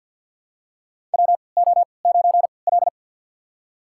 Morse code, heard as a keyed tone, about 50 wpm, spaced as for 28 wpm, 700 Hz. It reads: WO9L
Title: USA Call Signs - 4 Characters - 3x Character Spacing - 50wpm